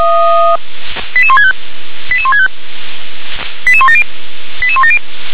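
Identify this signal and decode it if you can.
Five-tone selective call (ZVEI), heard as quick runs of five tones, tones of about 70 ms. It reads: 88165 88165 88170 88170
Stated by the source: ZVEI five-tone selective call 88220 received with an SDR.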